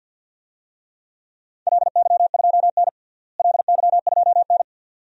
Morse code, heard as a keyed tone, about 52 wpm, 700 Hz. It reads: LY2N LY2N
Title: DXCC-100 Call Signs - 4 Characters - Repeated 2x - 50wpm